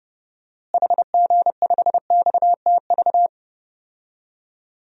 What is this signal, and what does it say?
Morse code, heard as a keyed tone, about 30 wpm, 700 Hz. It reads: HG5XTV
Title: DXCC-100 Call Signs - 6 Characters - Repeated 2x - 30wpm